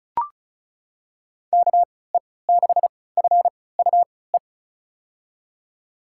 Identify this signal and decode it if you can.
Morse code, heard as a keyed tone, about 36 wpm, spaced as for 20 wpm, 700 Hz. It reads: KE6FUE
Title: USA Call Signs - 6 Characters - 3x Character Spacing - 35wpm